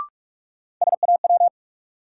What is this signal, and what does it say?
Morse code, heard as a keyed tone, about 46 wpm, 700 Hz. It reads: SAW
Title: Sentences from Top 300 Words - New Word Drill - 45wpm